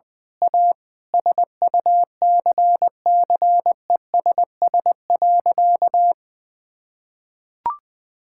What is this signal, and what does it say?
Morse code, heard as a keyed tone, about 19 wpm, 700 Hz. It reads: A SUCCESS.
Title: Sentences with 9-Word Phrases - 20wpm